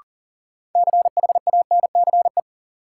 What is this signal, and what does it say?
Morse code, heard as a keyed tone, about 40 wpm, 700 Hz. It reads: CHANCE
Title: Sets of 3 Words - New Word Drill - 40wpm